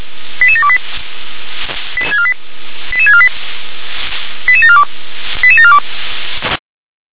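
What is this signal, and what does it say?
Five-tone selective call (ZVEI), heard as quick runs of five tones, tones of about 70 ms. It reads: 88728 88648 88742 88742
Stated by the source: Five-tone ZVEI selective call received by SDR.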